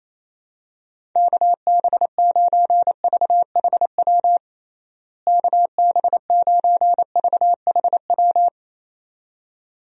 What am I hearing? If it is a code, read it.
Morse code, heard as a keyed tone, about 27 wpm, 700 Hz. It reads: KB9VHW KB9VHW
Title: USA Call Signs - 6 Characters - Repeated 2x - 28wpm